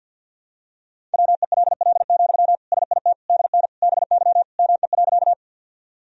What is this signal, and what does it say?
Morse code, heard as a keyed tone, about 48 wpm, 700 Hz. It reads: WELL, SIT DN BY ME.
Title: Sentences from Top 500 Words and Abbreviations 50wpm